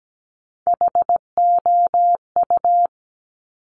Morse code, heard as a keyed tone, about 17 wpm, 700 Hz. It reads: HOU
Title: Top Three Letter Combinations in Words 17wpm